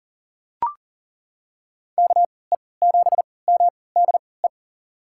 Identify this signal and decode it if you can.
Morse code, heard as a keyed tone, about 37 wpm, spaced as for 22 wpm, 700 Hz. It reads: KE7MDE
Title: USA Call Signs - 6 Characters - 3x Character Spacing - 40wpm